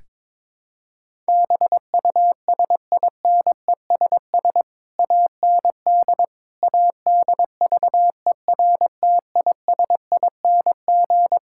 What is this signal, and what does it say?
Morse code, heard as a keyed tone, about 22 wpm, 700 Hz. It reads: BUSINESS AND ADVERTISING
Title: Sentences from Top 1000 Words - Review Part 6 - 22wpm